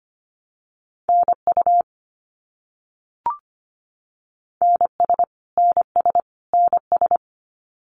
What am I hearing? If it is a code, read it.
Morse code, heard as a keyed tone, about 25 wpm, 700 Hz. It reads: NU NS NS NS
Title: Canadian Provinces and Territories - Repeated 3x - 25wpm